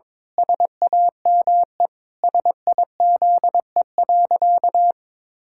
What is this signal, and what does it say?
Morse code, heard as a keyed tone, about 22 wpm, 700 Hz. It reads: SAME SIZE.
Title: Pangram Challenge v2 - 22wpm